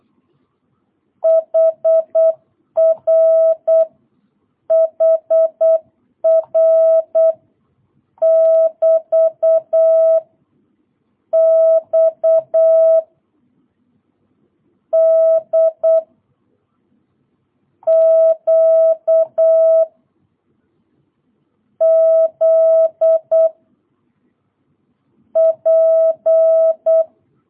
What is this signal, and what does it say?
Morse code, heard as a keyed tone, about 8 wpm, 660 Hz. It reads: HR HR <BT> X D Q Z P